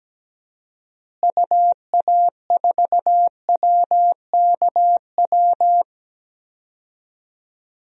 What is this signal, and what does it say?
Morse code, heard as a keyed tone, about 17 wpm, 700 Hz. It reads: UA4WKW
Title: DXCC-100 Call Signs - 6 Characters - Repeated 3x - 17wpm